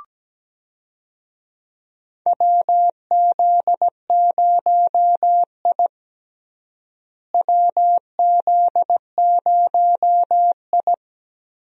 Morse code, heard as a keyed tone, about 17 wpm, 700 Hz. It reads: WZ0I WZ0I
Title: USA Call Signs - 4 Characters - Repeated 3x - 17wpm